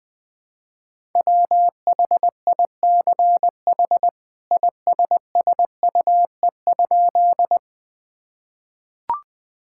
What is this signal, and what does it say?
Morse code, heard as a keyed tone, about 20 wpm, 700 Hz. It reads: WHICH ISSUE?
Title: Sentences from Top 1000 Words - Review Part 6 - 20wpm